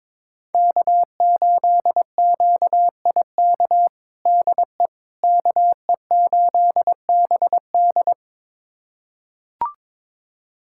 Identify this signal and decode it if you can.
Morse code, heard as a keyed tone, about 22 wpm, 700 Hz. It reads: K8QIK DE KE8BD
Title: USA Callsign DE Callsign - 5 Characters - 22wpm